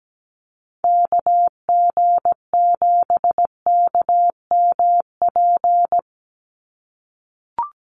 Morse code, heard as a keyed tone, about 17 wpm, 700 Hz. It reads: KG7KMP
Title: USA Call Signs 17wpm